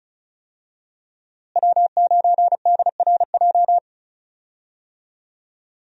Morse code, heard as a keyed tone, about 34 wpm, 700 Hz. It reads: W9DRJ
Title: USA Call Signs - 5 Characters - Repeated 3x - 35wpm